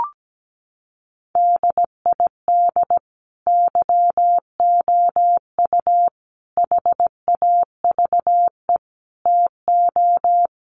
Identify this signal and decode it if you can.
Morse code, heard as a keyed tone, about 17 wpm, 700 Hz. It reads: DID YOU HAVE TO
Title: Sentences from Top 1500 Words 17wpm